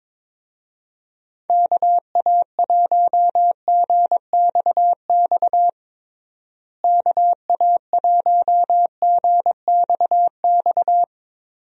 Morse code, heard as a keyed tone, about 22 wpm, 700 Hz. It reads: KA1GXX KA1GXX
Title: USA Call Signs - 6 Characters - Repeated 2x - 22wpm